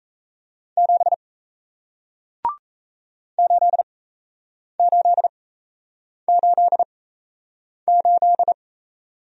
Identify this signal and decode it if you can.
Morse code, heard as a keyed tone, about 37 wpm, spaced as for 8 wpm, 700 Hz. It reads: 7 8888